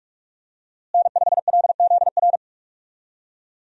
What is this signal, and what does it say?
Morse code, heard as a keyed tone, about 46 wpm, 700 Hz. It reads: N5LZR
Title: USA Call Signs - 5 Characters - Repeated 3x - 45wpm